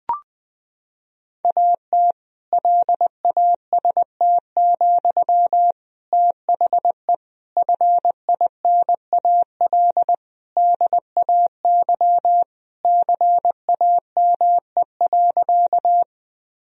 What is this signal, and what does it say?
Morse code, heard as a keyed tone, about 20 wpm, 700 Hz. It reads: AT LAST, THE FINAL DAY CAME.
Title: Sentences from Top 1000 Words - Review Part 2 - 20wpm